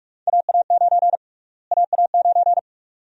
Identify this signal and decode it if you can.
Morse code, heard as a keyed tone, about 46 wpm, 700 Hz. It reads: AA9 AA9
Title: USA Call Sign Prefixes - Repeated 2x - 45wpm